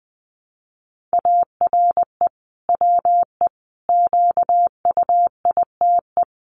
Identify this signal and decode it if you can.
Morse code, heard as a keyed tone, about 20 wpm, 700 Hz. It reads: ARE WE QUITE
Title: Sentences from Top 1000 Words 20wpm